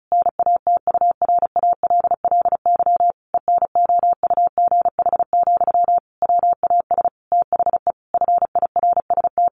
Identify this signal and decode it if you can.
Morse code, heard as a keyed tone, about 34 wpm, 700 Hz. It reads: NATURALLY ENOUGH, WAS THE FIRST
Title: Sentences from Top 1300 Words 35wpm